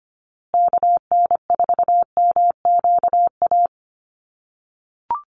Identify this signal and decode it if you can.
Morse code, heard as a keyed tone, about 25 wpm, 700 Hz. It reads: KN4MQA
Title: USA Call Signs Repeated 3x 25wpm